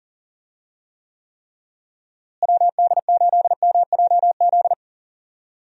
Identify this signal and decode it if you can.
Morse code, heard as a keyed tone, about 40 wpm, 700 Hz. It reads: WD8MJZ
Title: USA Call Signs - 6 Characters - Repeated 3x - 40wpm